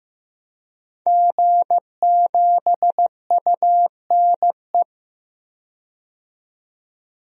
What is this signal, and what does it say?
Morse code, heard as a keyed tone, about 15 wpm, 700 Hz. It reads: G7UNE
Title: DXCC-100 Call Signs - 5 Characters - 15wpm